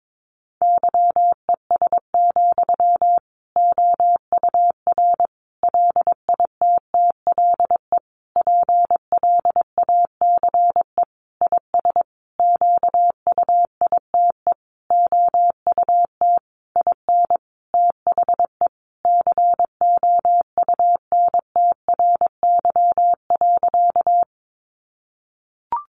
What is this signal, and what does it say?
Morse code, heard as a keyed tone, about 22 wpm, 700 Hz. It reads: YES, OUR LITTLE PLACE IS QUITE OUT IN THE COUNTRY.
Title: Sentences from Top 700 Words 22wpm